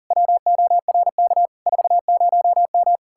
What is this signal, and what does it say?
Morse code, heard as a keyed tone, about 49 wpm, 700 Hz. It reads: WORK 40M
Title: Speed-Racing Common QSO Phrases - 40wpm